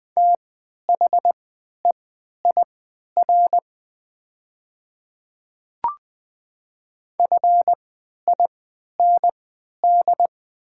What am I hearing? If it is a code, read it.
Morse code, heard as a keyed tone, about 20 wpm, spaced as for 11 wpm, 690 Hz. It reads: THEIR FIND
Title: 3x Character Spacing - Top 100 Words - 20wpm